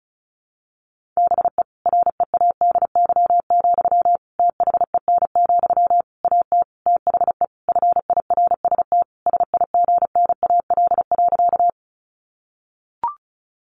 Morse code, heard as a keyed tone, about 34 wpm, 700 Hz. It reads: BE READY, THEN, AT THE FIRST SIGNAL.
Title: Sentences from Top 1400 Words 35wpm